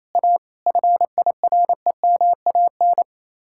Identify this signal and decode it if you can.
Morse code, heard as a keyed tone, about 33 wpm, 700 Hz. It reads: A FIREMAN
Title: Speed-Racing Common QSO Phrases - 28wpm